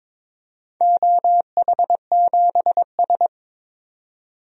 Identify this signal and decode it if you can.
Morse code, heard as a keyed tone, about 23 wpm, 700 Hz. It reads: OH7S
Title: DXCC-100 Call Signs - 4 Characters - 22wpm